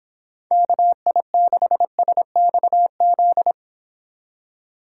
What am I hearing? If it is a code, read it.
Morse code, heard as a keyed tone, about 25 wpm, 700 Hz. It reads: KI6SXZ